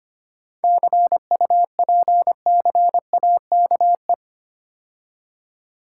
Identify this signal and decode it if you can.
Morse code, heard as a keyed tone, about 29 wpm, 700 Hz. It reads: CUPCAKE
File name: practice / 25wpm / Difficult Words - Speed-Racing with Hints - 25wpm